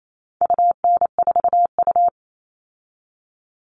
Morse code, heard as a keyed tone, about 28 wpm, 700 Hz. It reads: UN4U
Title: All Cut Numbers - 4-Digit 28wpm